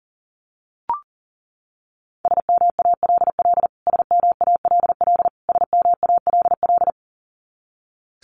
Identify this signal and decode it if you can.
Morse code, heard as a keyed tone, about 40 wpm, 700 Hz. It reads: SMALL SMALL SMALL
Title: Top 500 Words Repeated 40wpm